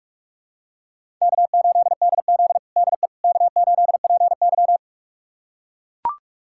Morse code, heard as a keyed tone, about 42 wpm, 700 Hz. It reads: K8DZ DE K8PY
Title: USA Callsign DE Callsign - 4 Characters - 45wpm